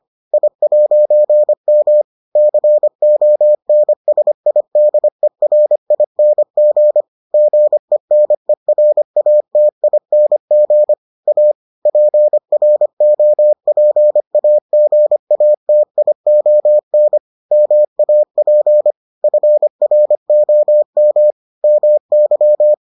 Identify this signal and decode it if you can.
Morse code, heard as a keyed tone, about 25 wpm, 600 Hz. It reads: I'M CONSIDERING GENERATING A PROPAGATION MAP FROM MY